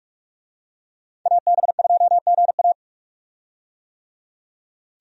Morse code, heard as a keyed tone, about 46 wpm, 700 Hz. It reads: AB2GA
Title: USA Call Signs - 5 Characters - 45wpm